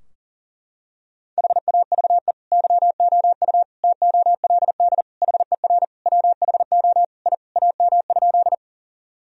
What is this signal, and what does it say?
Morse code, heard as a keyed tone, about 40 wpm, 700 Hz. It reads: HAVE YOU TOLD HER WHO I AM?
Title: Sentences from Top 1000 Words - Review Part 2 - 40wpm